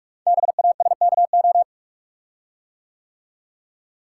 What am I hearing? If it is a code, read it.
Morse code, heard as a keyed tone, about 44 wpm, 690 Hz. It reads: BASKO
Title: Code Groups - 5 Letters - 45wpm